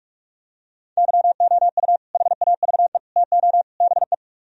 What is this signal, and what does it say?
Morse code, heard as a keyed tone, about 45 wpm, 700 Hz. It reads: YOU HAVE TO BE